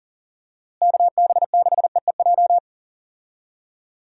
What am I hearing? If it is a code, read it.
Morse code, heard as a keyed tone, about 40 wpm, 700 Hz. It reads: KB6EEJ
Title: USA Call Signs - 6 Characters - Repeated 2x - 40wpm